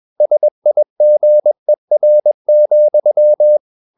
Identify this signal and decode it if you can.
Morse code, heard as a keyed tone, about 21 wpm, 600 Hz. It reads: SIGER,
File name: Ferie QRS 21wpm